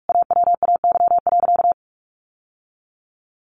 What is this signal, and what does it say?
Morse code, heard as a keyed tone, about 44 wpm, 700 Hz. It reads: AWAY.
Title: Sentences from Top 1200 Words 45wpm